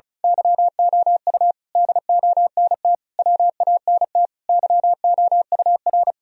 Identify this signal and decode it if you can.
Morse code, heard as a keyed tone, about 34 wpm, 700 Hz. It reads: YOU DONT WANT YOUR
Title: Sentences from Top 1000 Words - Review Part 2 - 35wpm